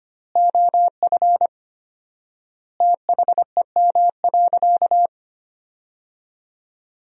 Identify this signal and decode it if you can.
Morse code, heard as a keyed tone, about 25 wpm, 700 Hz. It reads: OF THEM.
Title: 4x Word-Spacing - Sentences from Top 100 Words - 25wpm